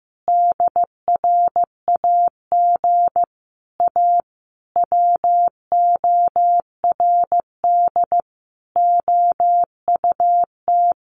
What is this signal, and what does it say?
Morse code, heard as a keyed tone, about 15 wpm, 700 Hz. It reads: DRAG A WORD OUT
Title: Sentences from Top 1500 Words 15wpm